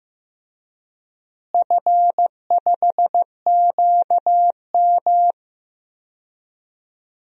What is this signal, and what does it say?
Morse code, heard as a keyed tone, about 15 wpm, 700 Hz. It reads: F5QM
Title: DXCC-100 Call Signs - 4 Characters - Repeated 3x - 15wpm